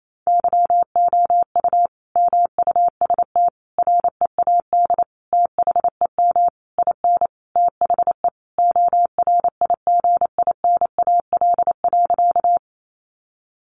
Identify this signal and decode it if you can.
Morse code, heard as a keyed tone, about 28 wpm, 700 Hz. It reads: YOU MUST READ THEM IN THE ORIGINAL.
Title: Sentences from Top 900 Words 28wpm